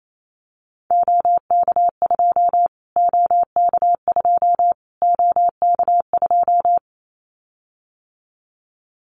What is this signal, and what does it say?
Morse code, heard as a keyed tone, about 28 wpm, 700 Hz. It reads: OK2 OK2 OK2
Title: Callsign Prefixes for Top 25 DXCC - Repeated 3x - 28wpm